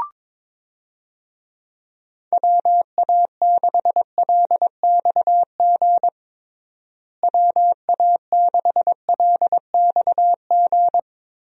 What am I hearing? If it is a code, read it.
Morse code, heard as a keyed tone, about 22 wpm, 700 Hz. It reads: WA6LXG WA6LXG
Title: USA Call Signs - 6 Characters - Repeated 3x - 22wpm